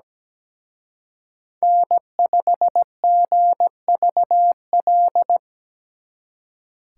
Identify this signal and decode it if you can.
Morse code, heard as a keyed tone, about 20 wpm, 700 Hz. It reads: N5GVL
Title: Speed Racing - USA 1x3 Call Signs 17wpm